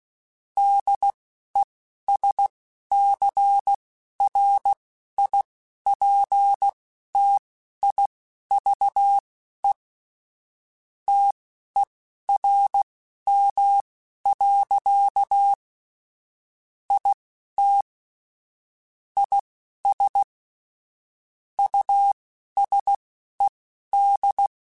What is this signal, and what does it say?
Morse code, heard as a keyed tone, about 16 wpm, spaced as for 11 wpm, 800 Hz. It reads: DESCRIPTIVE TERM. IT IS USED